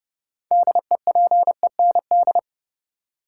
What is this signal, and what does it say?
Morse code, heard as a keyed tone, about 30 wpm, 700 Hz. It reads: DEPEND
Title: Six Letter Words - 30wpm